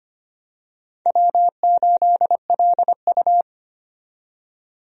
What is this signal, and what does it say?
Morse code, heard as a keyed tone, about 25 wpm, 700 Hz. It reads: W8LU
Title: USA Call Signs - 4 Characters - 25wpm